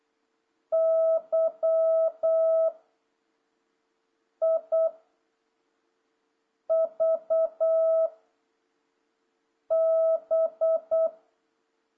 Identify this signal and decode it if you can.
Morse code, heard as a keyed tone, about 8 wpm, 660 Hz. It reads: Y I V B
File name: jh8cbh / 2026cw